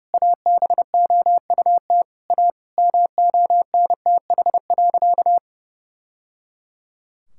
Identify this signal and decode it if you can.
Morse code, heard as a keyed tone, about 30 wpm, 700 Hz. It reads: ABOUT A MONTH.
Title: Sentences from Top 1000 Words - Review Part 6 - 30wpm